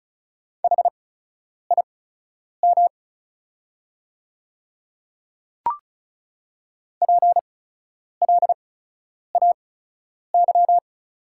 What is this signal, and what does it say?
Morse code, heard as a keyed tone, about 36 wpm, spaced as for 10 wpm, 690 Hz. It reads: HIM PLAY
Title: 8x Character Spacing - Top 200 Words - 35wpm